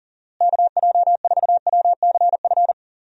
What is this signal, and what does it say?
Morse code, heard as a keyed tone, about 40 wpm, 700 Hz. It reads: KJ4WCF
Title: USA Call Signs - 6 Characters - Repeated 3x - 40wpm